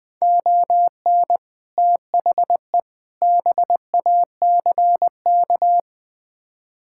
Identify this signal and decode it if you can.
Morse code, heard as a keyed tone, about 20 wpm, 700 Hz. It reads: ON THE BACK
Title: Build A Phrase - 5-Word Phrases - 20wpm